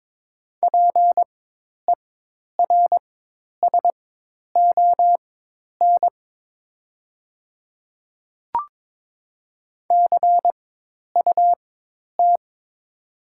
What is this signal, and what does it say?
Morse code, heard as a keyed tone, about 22 wpm, spaced as for 10 wpm, 700 Hz. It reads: PERSON CUT
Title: 4x Character Spacing - Top 500 Words - 22wpm